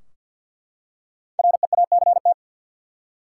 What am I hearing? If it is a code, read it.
Morse code, heard as a keyed tone, about 51 wpm, 700 Hz. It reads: REACT
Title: Five Letter Words - 50wpm